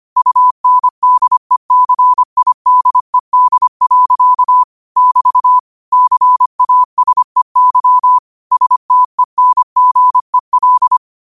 Morse code, heard as a keyed tone, about 25 wpm, 1000 Hz. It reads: UNDECIDED. <BT> CASEY STENGEL
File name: qotdfiles / 25wpm